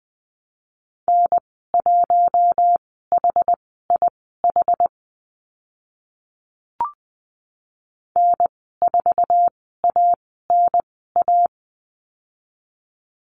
Morse code, heard as a keyed tone, about 20 wpm, spaced as for 14 wpm, 700 Hz. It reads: N1HIH N4ANA
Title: 062 Lesson 11g - Letter h - Call Signs 20wpm